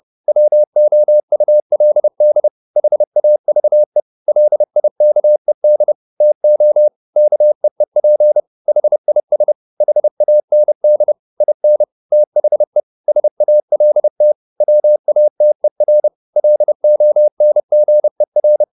Morse code, heard as a keyed tone, about 30 wpm, 600 Hz. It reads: WOULD HAVE LIKED TO KEEP HIS HAND IN THE SALT WATER LONGER
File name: the old man and the sea / the old man and the sea 1 30wpm